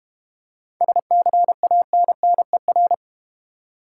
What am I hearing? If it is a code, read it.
Morse code, heard as a keyed tone, about 31 wpm, 700 Hz. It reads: SCANNER
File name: practice / 25wpm / Difficult Words - Speed-Racing with Hints - 25wpm